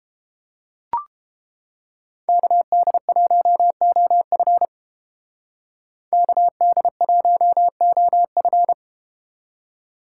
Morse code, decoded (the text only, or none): KD1OF KD1OF